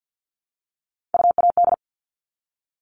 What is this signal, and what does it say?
Morse code, heard as a keyed tone, about 50 wpm, 700 Hz. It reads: UAD
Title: All Cut Numbers - 3-Digit 50wpm